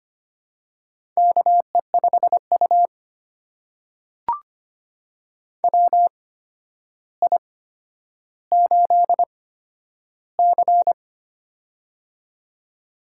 Morse code, heard as a keyed tone, about 25 wpm, spaced as for 11 wpm, 700 Hz. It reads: KE5U WI8C
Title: USA Call Signs - 4 Characters - 8x Character Spacing - 25wpm